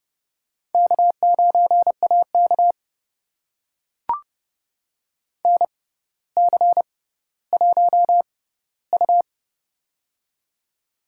Morse code, heard as a keyed tone, about 30 wpm, spaced as for 15 wpm, 700 Hz. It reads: K9AK NC1U
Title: USA Call Signs - 4 Characters - 6x Character Spacing - 30wpm